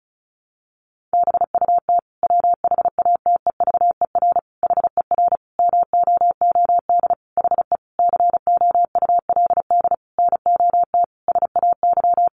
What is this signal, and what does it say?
Morse code, heard as a keyed tone, about 34 wpm, 700 Hz. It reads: BUT WHATEVER HER MOOD HE COULD NOT SAY
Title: Sentences from Top 1100 Words 35wpm